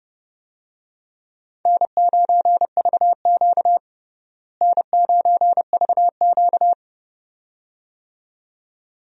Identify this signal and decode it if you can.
Morse code, heard as a keyed tone, about 30 wpm, 700 Hz. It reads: N9VQ N9VQ
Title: USA Call Signs - 4 Characters - Repeated 2x - 30wpm